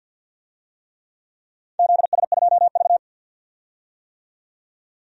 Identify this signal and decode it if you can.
Morse code, heard as a keyed tone, about 51 wpm, 700 Hz. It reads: ZS2V
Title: DXCC-100 Call Signs - 4 Characters - 50wpm